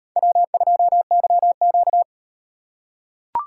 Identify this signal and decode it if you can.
Morse code, heard as a keyed tone, about 38 wpm, 700 Hz. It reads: W2YQ